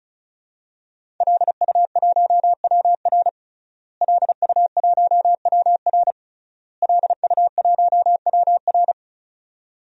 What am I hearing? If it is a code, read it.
Morse code, heard as a keyed tone, about 36 wpm, 700 Hz. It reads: LU1WR LU1WR LU1WR
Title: DXCC-100 Call Signs - 5 Characters - Repeated 3x - 35wpm